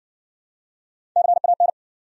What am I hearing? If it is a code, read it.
Morse code, heard as a keyed tone, about 59 wpm, 700 Hz. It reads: CAN